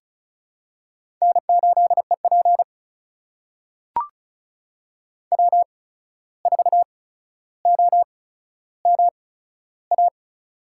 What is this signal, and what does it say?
Morse code, heard as a keyed tone, about 36 wpm, spaced as for 14 wpm, 700 Hz. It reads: N8EP W4OMA
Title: USA Call Signs - 8x Character Spacing - 35wpm